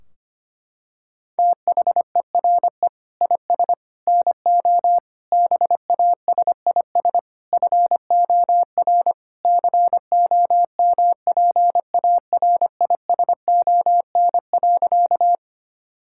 Morse code, heard as a keyed tone, about 25 wpm, 700 Hz. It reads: THERE IS NO BASIS FOR COMPARISON.
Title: Sentences from Top 1600 Words 25wpm